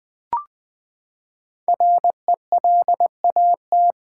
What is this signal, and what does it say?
Morse code, heard as a keyed tone, about 20 wpm, 700 Hz. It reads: RELAT
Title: Build A Word - 5 Letters At A Time - 12-Letter Words - 20wpm